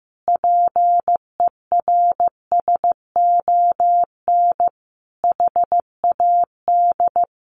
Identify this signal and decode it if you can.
Morse code, heard as a keyed tone, about 14 wpm, 700 Hz. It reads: PERSON HAD
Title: Sentences from Top 1100 Words 15wpm